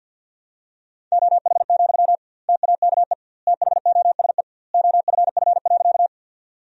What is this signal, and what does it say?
Morse code, heard as a keyed tone, about 51 wpm, 700 Hz. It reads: OH, TAKE THOSE OFF.
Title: 2x Word-Spacing - Sentences from Top 500 Words - 50wpm